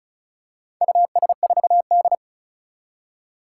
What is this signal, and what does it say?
Morse code, heard as a keyed tone, about 36 wpm, 700 Hz. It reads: US4D
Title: Four Challenging Letter-Number Combinations 35wpm